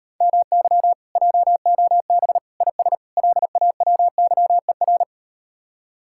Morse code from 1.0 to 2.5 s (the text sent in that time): JOB